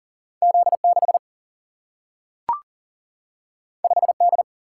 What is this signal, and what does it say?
Morse code, heard as a keyed tone, about 40 wpm, 700 Hz. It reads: Z6 5D
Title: Single-Letter Single-Number 40wpm